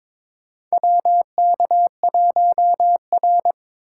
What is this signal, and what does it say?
Morse code, heard as a keyed tone, about 21 wpm, 700 Hz. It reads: WK1R
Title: USA Call Signs - 4 Characters - 4x Character Spacing - 22wpm